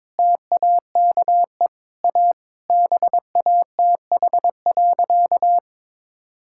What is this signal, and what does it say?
Morse code, heard as a keyed tone, about 22 wpm, 700 Hz. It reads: TAKE A BATH.
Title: Sentences from Top 1600 Words 22wpm